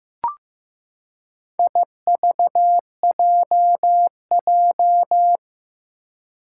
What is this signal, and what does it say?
Morse code, heard as a keyed tone, about 15 wpm, 700 Hz. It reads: IVJJ
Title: Four Challenging Letter-Number Combinations 15wpm